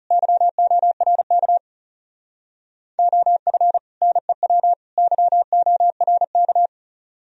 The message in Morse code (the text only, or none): YORK OF NEW YORK